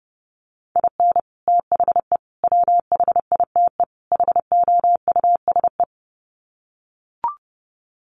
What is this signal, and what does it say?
Morse code, heard as a keyed tone, about 28 wpm, 700 Hz. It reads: IN THE WHITE HOUSE
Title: Sets of 4 Words 30wpm